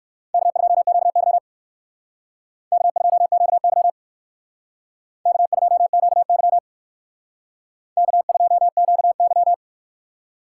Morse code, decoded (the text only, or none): K2QY K2QY K2QY K2QY